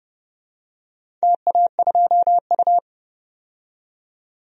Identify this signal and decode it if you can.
Morse code, heard as a keyed tone, about 32 wpm, 700 Hz. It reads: TA2U
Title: DXCC-100 Call Signs - 4 Characters - 30wpm